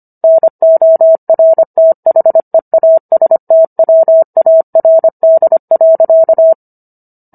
Morse code, heard as a keyed tone, about 25 wpm, 650 Hz. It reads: NORTHEASTWARD.